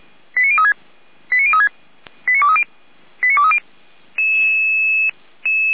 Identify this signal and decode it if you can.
Five-tone selective call (ZVEI), heard as quick runs of five tones, tones of about 70 ms. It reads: 89036 89036 89230 89230